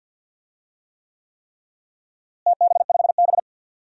Morse code, heard as a keyed tone, about 47 wpm, 700 Hz. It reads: TB5B
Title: DXCC-100 Call Signs - 4 Characters - Repeated 3x - 50wpm